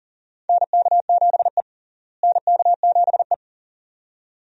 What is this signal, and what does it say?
Morse code, heard as a keyed tone, about 40 wpm, 700 Hz. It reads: NK7E NK7E
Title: USA Call Signs - 4 Characters - Repeated 2x - 40wpm